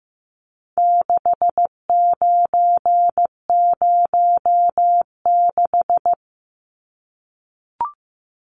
Morse code, heard as a keyed tone, about 15 wpm, 700 Hz. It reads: 6906
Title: Numbers - 4-Digit 15wpm